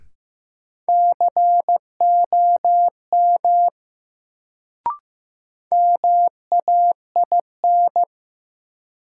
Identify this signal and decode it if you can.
Morse code, heard as a keyed tone, about 15 wpm, 700 Hz. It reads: COM MAIN
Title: Sentences from Top 1300 Words - New Word Drill Alt - 15wpm